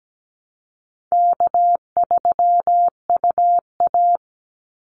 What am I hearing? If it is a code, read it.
Morse code, heard as a keyed tone, about 17 wpm, 700 Hz. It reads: K3UA
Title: Easy Call Signs - 17wpm